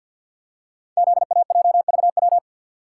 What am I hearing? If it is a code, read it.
Morse code, heard as a keyed tone, about 51 wpm, 700 Hz. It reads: ZAJVW
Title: Code Groups - 5 Letters - 50wpm